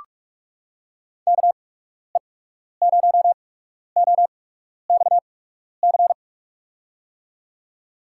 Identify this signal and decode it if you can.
Morse code, heard as a keyed tone, about 43 wpm, spaced as for 12 wpm, 700 Hz. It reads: KE0OXC
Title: USA Call Signs - 6 Characters - 8x Character Spacing - 45wpm